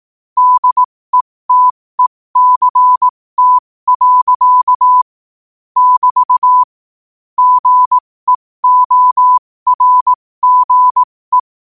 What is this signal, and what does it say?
Morse code, heard as a keyed tone, about 18 wpm, spaced as for 15 wpm, 1000 Hz. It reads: DETECT. <BT> GEORGE
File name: qotdfiles / 15wpm